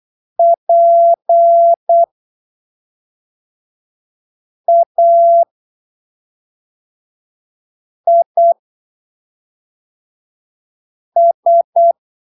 Morse code, heard as a keyed tone, about 8 wpm, spaced as for 3 wpm, 680 Hz. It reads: PAIS
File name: arquivos / texto / texto 22 08ppm